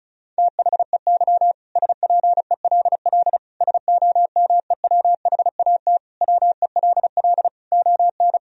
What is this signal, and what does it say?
Morse code, heard as a keyed tone, about 36 wpm, 700 Hz. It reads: THEY SPELL SOMEWHAT WELL ON
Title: Sentences from Top 1000 Words - Review Part 1 - 35wpm